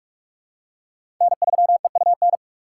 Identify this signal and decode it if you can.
Morse code, heard as a keyed tone, about 46 wpm, 700 Hz. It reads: N3EUN
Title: USA Call Signs - 5 Characters - 4x Character Spacing - 45wpm